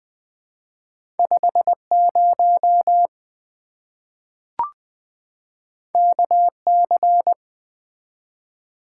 Morse code, heard as a keyed tone, about 20 wpm, 700 Hz. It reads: 50 KC
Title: Two Letter-Number Combinations - 20wpm